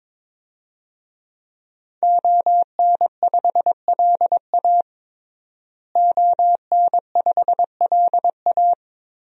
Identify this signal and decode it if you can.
Morse code, heard as a keyed tone, about 22 wpm, 700 Hz. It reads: ON5LA ON5LA
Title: DXCC-100 Call Signs - 5 Characters - Repeated 3x - 22wpm